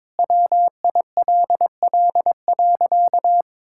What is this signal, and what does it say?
Morse code, heard as a keyed tone, about 21 wpm, 700 Hz. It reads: WILL.
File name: Sentences with 9-Word Phrases - 22wpm